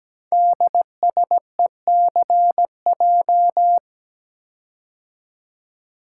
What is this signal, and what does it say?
Morse code, heard as a keyed tone, about 17 wpm, 700 Hz. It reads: DSECJ
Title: Code Groups - 5 Letters - 17wpm